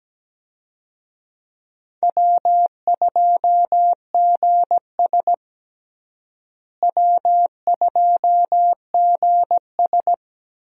Morse code, heard as a keyed tone, about 17 wpm, 700 Hz. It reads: W2GS W2GS
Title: USA Call Signs - 4 Characters - Repeated 2x - 17wpm